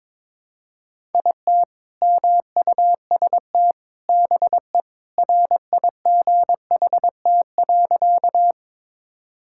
Morse code, heard as a keyed tone, about 22 wpm, 700 Hz. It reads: IT MUST BE RIGHT.
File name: practice / 22wpm / Sentences with 2-Word Phrases - 22wpm